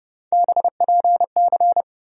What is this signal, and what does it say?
Morse code, heard as a keyed tone, about 30 wpm, 700 Hz. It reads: BPC
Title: USA Call Sign Postfixes - 3 Letters 30wpm